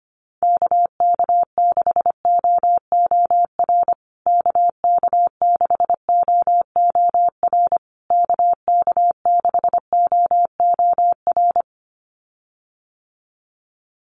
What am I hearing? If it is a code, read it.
Morse code, heard as a keyed tone, about 25 wpm, 700 Hz. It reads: KK6OOR KK6OOR KK6OOR
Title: USA Call Signs Repeated 3x 25wpm